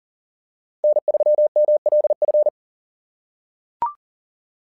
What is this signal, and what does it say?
Morse code, heard as a keyed tone, about 39 wpm, 600 Hz. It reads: N3MLF
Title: SOTA QSO Practice 40wpm